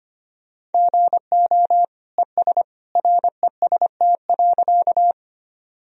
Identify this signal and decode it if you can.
Morse code, heard as a keyed tone, about 25 wpm, 700 Hz. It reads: GO ES REST.
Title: Sentences from Top 500 Words and Abbreviations 25wpm